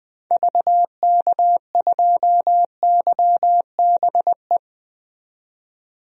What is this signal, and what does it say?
Morse code, heard as a keyed tone, about 20 wpm, 700 Hz. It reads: VK2YBE
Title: DXCC-100 Call Signs - 6 Characters - 20wpm